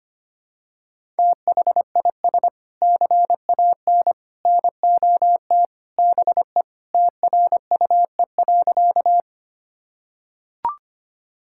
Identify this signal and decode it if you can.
Morse code, heard as a keyed tone, about 25 wpm, 700 Hz. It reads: THIS CAN NOT BE TRUE.
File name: practice / 25wpm / Sentences with 3-Word Phrases - 25wpm